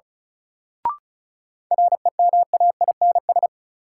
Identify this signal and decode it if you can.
Morse code, heard as a keyed tone, about 36 wpm, 700 Hz. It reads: REMAINS
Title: Sets of 5 Words Encore - New Word Drill - 35wpm